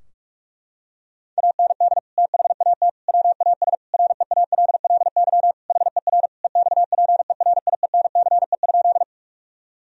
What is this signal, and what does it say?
Morse code, heard as a keyed tone, about 46 wpm, 700 Hz. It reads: AND THAT WAS REALLY HER EXPERIENCE?
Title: Sentences from Top 1000 Words - Review Part 3 - 45wpm